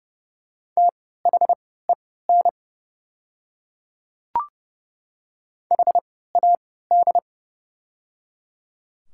Morse code, heard as a keyed tone, about 30 wpm, spaced as for 17 wpm, 700 Hz. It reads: THEN HAD
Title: 3x Character Spacing - Top 100 Words - 30wpm